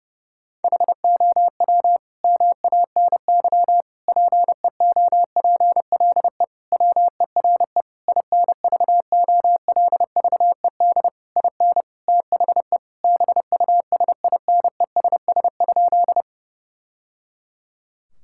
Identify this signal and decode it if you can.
Morse code, heard as a keyed tone, about 30 wpm, 700 Hz. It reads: HOW MANY PEOPLE WERE INVOLVED IN THE BUSINESS?
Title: Sentences from Top 1000 Words - Review Part 5 - 30wpm